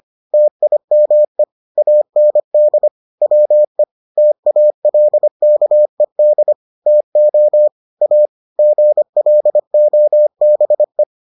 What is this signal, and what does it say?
Morse code, heard as a keyed tone, about 25 wpm, 600 Hz. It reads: TIME AND WE TALKED TO A GLOBE